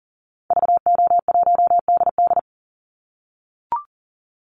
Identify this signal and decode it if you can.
Morse code, heard as a keyed tone, about 40 wpm, 700 Hz. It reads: VO1DD
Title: Call Signs - 40wpm